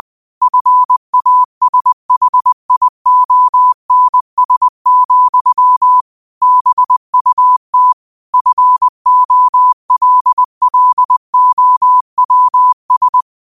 Morse code, heard as a keyed tone, about 20 wpm, 1000 Hz. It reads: FASHIONS, BUT FOLLOWS
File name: qotdfiles / 20wpm